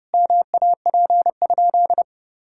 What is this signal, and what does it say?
Morse code, heard as a keyed tone, about 32 wpm, 700 Hz. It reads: MAP?
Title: Sentences from Top 1000 Words - Review Part 5 - 30wpm